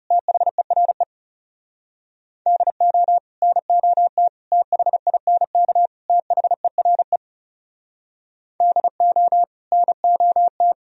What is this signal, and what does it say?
Morse code, heard as a keyed tone, about 40 wpm, 700 Hz. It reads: THERE DO NOT THINK THERE DO NOT